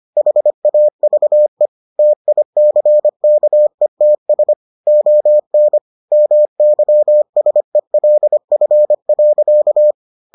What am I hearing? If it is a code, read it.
Morse code, heard as a keyed tone, about 25 wpm, 600 Hz. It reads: HAVE TICKETS ON MYSELF.